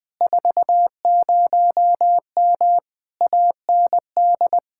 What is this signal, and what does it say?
Morse code, heard as a keyed tone, about 20 wpm, 700 Hz. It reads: 40M AND
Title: Speed-Racing Common QSO Phrases - 20wpm